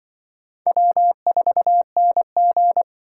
Morse code, heard as a keyed tone, about 24 wpm, 700 Hz. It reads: W4NG
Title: Speed Racing - USA 1x2 Call Signs 20wpm